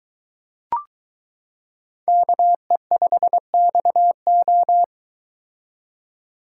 Morse code, heard as a keyed tone, about 22 wpm, 700 Hz. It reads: KE5XO